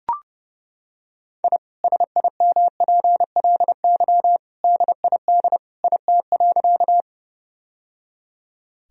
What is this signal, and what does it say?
Morse code, heard as a keyed tone, about 30 wpm, 700 Hz. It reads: I SIMPLY DID IT.
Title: Sentences from Top 1000 Words - Review Part 4 - 30wpm